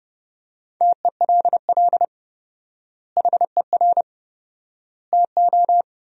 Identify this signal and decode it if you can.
Morse code, heard as a keyed tone, about 30 wpm, 700 Hz. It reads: TELL HER TO
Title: 4x Word-Spacing - Sentences from Top 300 Words - 30wpm